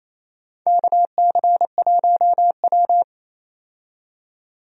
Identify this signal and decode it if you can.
Morse code, heard as a keyed tone, about 28 wpm, 700 Hz. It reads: KC1W
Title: USA Call Signs - 4 Characters - 28wpm